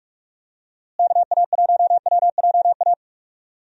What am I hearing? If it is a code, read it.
Morse code, heard as a keyed tone, about 46 wpm, 700 Hz. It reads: KA1WJA
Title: USA Call Signs - 6 Characters - Repeated 3x - 45wpm